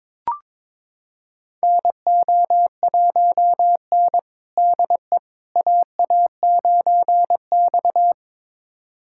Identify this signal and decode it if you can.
Morse code, heard as a keyed tone, about 22 wpm, 700 Hz. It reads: NO1N DE AA9X
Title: USA Callsign DE Callsign - 4 Characters - 22wpm